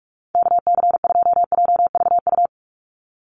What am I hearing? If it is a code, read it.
Morse code, heard as a keyed tone, about 44 wpm, 700 Hz. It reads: KC2JVU
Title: USA Call Signs 45wpm